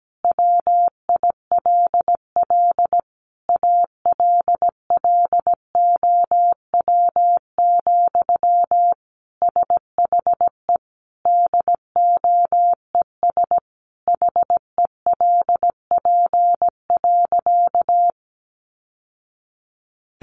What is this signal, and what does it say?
Morse code, heard as a keyed tone, about 17 wpm, 700 Hz. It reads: WILL ALLOW, SHE DOES HELP.
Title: Sentences from Top 700 Words 17wpm